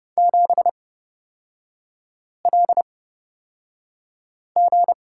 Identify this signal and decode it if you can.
Morse code, heard as a keyed tone, about 30 wpm, spaced as for 5 wpm, 700 Hz. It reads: 7LG
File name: Lesson 37e - Letter x - Three Characters 30wpm